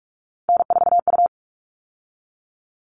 Morse code, heard as a keyed tone, about 44 wpm, 700 Hz. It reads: N4U
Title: All Cut Numbers - 3-Digit 45wpm